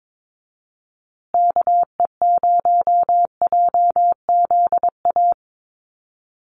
Morse code, heard as a keyed tone, about 22 wpm, 700 Hz. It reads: KE0JZA
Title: USA Call Signs Repeated 2x 22wpm